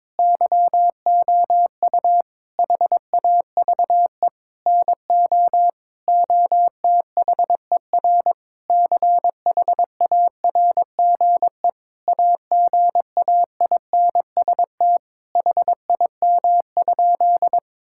Morse code, heard as a keyed tone, about 21 wpm, 700 Hz. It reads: YOU HAVE NO OTHER CHARGE AGAINST HIM?
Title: Sentences from Top 1000 Words - Review Part 6 - 22wpm